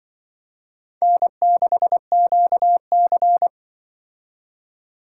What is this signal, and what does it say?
Morse code, heard as a keyed tone, about 24 wpm, 700 Hz. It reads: N6QC
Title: Speed Racing - USA 1x2 Call Signs 20wpm